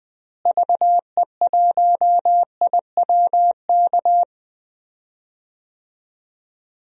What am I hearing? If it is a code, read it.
Morse code, heard as a keyed tone, about 21 wpm, 700 Hz. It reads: VE1IWK
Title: DXCC-100 Call Signs - 6 Characters - Repeated 2x - 20wpm